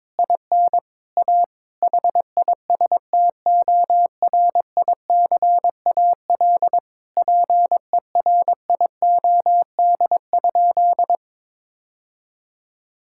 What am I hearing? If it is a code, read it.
Morse code, heard as a keyed tone, about 22 wpm, 700 Hz. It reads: IN A HISTORICAL PERIOD?
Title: Sentences from Top 1000 Words - Review Part 5 - 22wpm